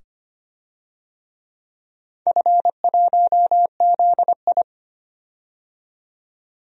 Morse code, heard as a keyed tone, about 25 wpm, 700 Hz. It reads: F1ZI
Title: DXCC-100 Call Signs - 4 Characters - 25wpm